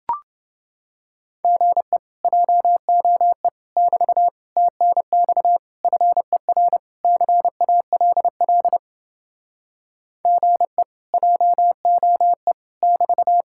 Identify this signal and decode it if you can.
Morse code, heard as a keyed tone, about 30 wpm, 700 Hz. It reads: GE JOE <BT> TNX FER CALL GE JOE <BT>